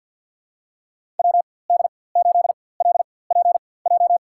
Morse code, heard as a keyed tone, about 52 wpm, spaced as for 24 wpm, 700 Hz. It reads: WD8LPJ
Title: USA Call Signs - 6 Characters - 4x Character Spacing - 50wpm